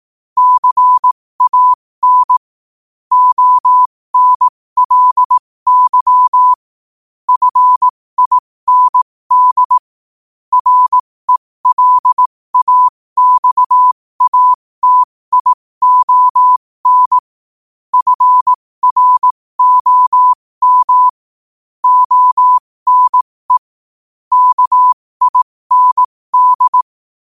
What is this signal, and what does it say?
Morse code, heard as a keyed tone, about 18 wpm, spaced as for 15 wpm, 1000 Hz. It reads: CAN ONLY FIND RELAXATION FROM ONE KIND
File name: qotdfiles / 15wpm